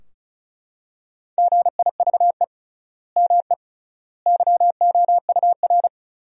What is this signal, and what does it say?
Morse code, heard as a keyed tone, about 34 wpm, 700 Hz. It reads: GIVE ME YOUR
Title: 3x Word-Spacing - Sentences from Top 200 Words - 35wpm